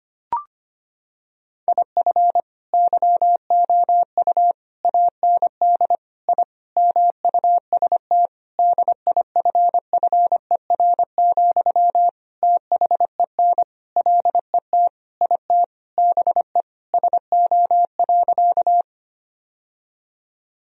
Morse code, heard as a keyed tone, about 25 wpm, 700 Hz. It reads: IF YOU AND I MUST DIFFER, THEN LET IT BE SO.
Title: Sentences from Top 1000 Words - Review Part 1 - 25wpm